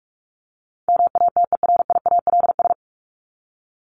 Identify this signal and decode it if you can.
Morse code, heard as a keyed tone, about 44 wpm, 700 Hz. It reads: MATERIALS
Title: ES:S Suffix - Top 100 Words with Base Words 45wpm